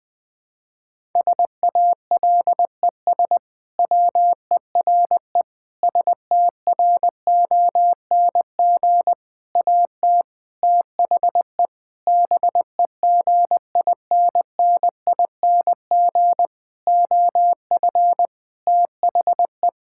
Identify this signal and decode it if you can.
Morse code, heard as a keyed tone, about 20 wpm, 700 Hz. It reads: SALES WERE STRONG AT THE BEGINNING OF THE
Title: Sentences with 6-Word Phrases - 20wpm